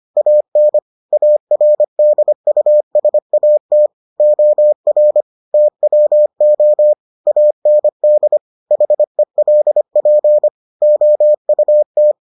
Morse code, heard as a keyed tone, about 25 wpm, 600 Hz. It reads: AN ARDUSAT OR TWO AND HELP OUT